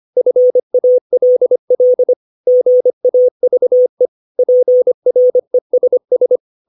Morse code, heard as a keyed tone, about 25 wpm, 500 Hz. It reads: FALL GAVE PRESS
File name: 632 words / 632 Words 25WPM